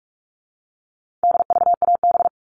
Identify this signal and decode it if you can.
Morse code, heard as a keyed tone, about 44 wpm, 700 Hz. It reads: DVAB
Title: All Cut Numbers - 4-Digit 45wpm